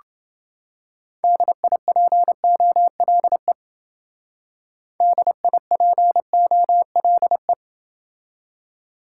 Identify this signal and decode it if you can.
Morse code, heard as a keyed tone, about 28 wpm, 700 Hz. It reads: DIPOLE DIPOLE